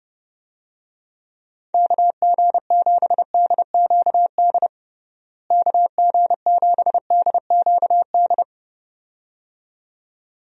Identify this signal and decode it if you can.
Morse code, heard as a keyed tone, about 30 wpm, 700 Hz. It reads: KG7DQD KG7DQD
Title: USA Call Signs - 6 Characters - Repeated 2x - 30wpm